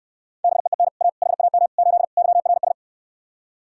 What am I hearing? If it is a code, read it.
Morse code, heard as a keyed tone, about 69 wpm, 700 Hz. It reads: BEEN A HAM 8 YRS